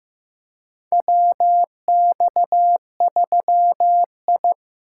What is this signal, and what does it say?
Morse code, heard as a keyed tone, about 15 wpm, 700 Hz. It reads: WX3I
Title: USA Call Signs - 8x Character Spacing - 15wpm